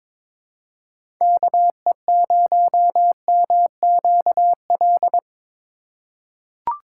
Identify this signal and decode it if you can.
Morse code, heard as a keyed tone, about 22 wpm, 700 Hz. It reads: KE0MQL
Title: USA Call Signs - 8x Character Spacing - 22wpm